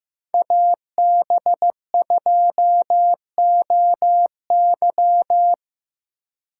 Morse code, heard as a keyed tone, about 17 wpm, 700 Hz. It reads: AB2OY
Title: Speed Racing - USA 2x2 Call Signs 15wpm